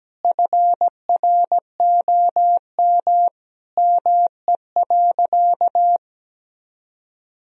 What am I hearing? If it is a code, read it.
Morse code, heard as a keyed tone, about 17 wpm, 700 Hz. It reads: FROM ME.
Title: Sentences from Top 1000 Words - Review Part 1 - 17wpm